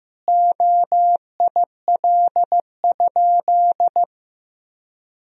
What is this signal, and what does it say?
Morse code, heard as a keyed tone, about 15 wpm, 700 Hz. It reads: OIL?
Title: Sentences from Top 1000 Words - Review Part 4 - 15wpm